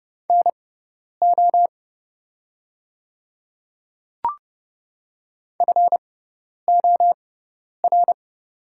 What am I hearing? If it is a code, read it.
Morse code, heard as a keyed tone, about 30 wpm, spaced as for 10 wpm, 700 Hz. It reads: NO FOR
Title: 6x Character Spacing - Top 100 Words - 30wpm